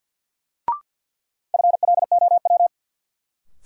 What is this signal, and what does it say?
Morse code, heard as a keyed tone, about 53 wpm, 700 Hz. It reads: FLOW
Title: Sentences from Top 1200 Words - New Word Drill Alt - 50wpm